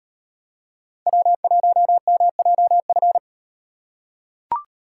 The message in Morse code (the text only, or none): W1MJF